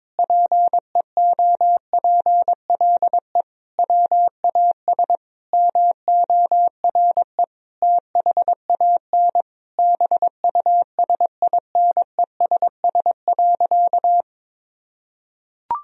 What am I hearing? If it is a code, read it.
Morse code, heard as a keyed tone, about 22 wpm, 700 Hz. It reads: PEOPLE WAS MORE THAN BUSINESS.
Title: Sentences from Top 1000 Words - Review Part 2 - 22wpm